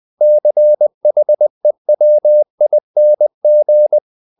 Morse code, heard as a keyed tone, about 20 wpm, 600 Hz. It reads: CHEWING
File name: the old man and the sea part 2 20wpm